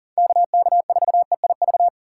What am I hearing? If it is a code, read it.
Morse code, heard as a keyed tone, about 52 wpm, 700 Hz. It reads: KK4EIV